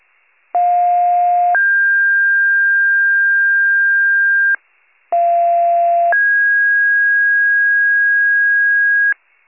Two-tone page, heard as a one-second tone, then a three-second tone, about 710 then 1600 Hz, about 690 then 1650 Hz. It is heard from 0.5 to 9.1 s.